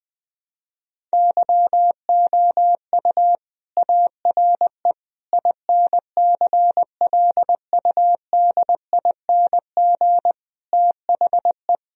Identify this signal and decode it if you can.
Morse code, heard as a keyed tone, about 20 wpm, 700 Hz. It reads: YOU ARE INCLUDING THE
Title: Sentences from Top 1000 Words - Review Part 4 - 20wpm